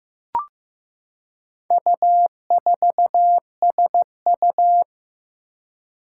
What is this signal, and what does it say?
Morse code, heard as a keyed tone, about 15 wpm, 700 Hz. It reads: U4SU
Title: Four Challenging Letter-Number Combinations 15wpm